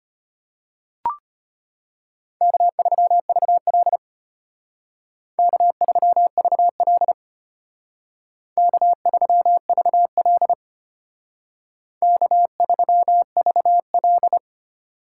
Morse code, heard as a keyed tone, about 32 wpm, 700 Hz. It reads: K3VL K3VL K3VL K3VL